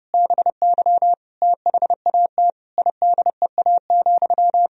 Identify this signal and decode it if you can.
Morse code, heard as a keyed tone, about 30 wpm, 700 Hz. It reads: BY THAT IDEA,
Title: Sentences from Top 1000 Words - Review Part 2 - 30wpm